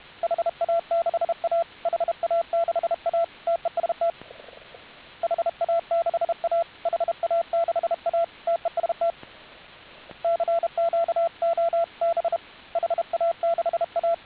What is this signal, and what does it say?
Morse code, heard as a keyed tone, about 31 wpm, 690 Hz, with radio noise. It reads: HA6A HA6A TEST HA6A HA6A TEST CQ OB HA6A